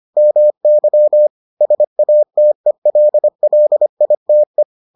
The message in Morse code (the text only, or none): MY SATELLITE